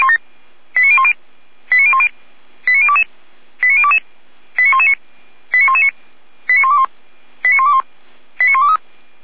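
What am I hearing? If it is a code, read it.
Five-tone selective call (ZVEI), heard as quick runs of five tones, tones of about 70 ms. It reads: 89010 89010 89022 89022 89119 89119 89121 89121 89123
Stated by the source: ZVEI five-tone call received on an SDR.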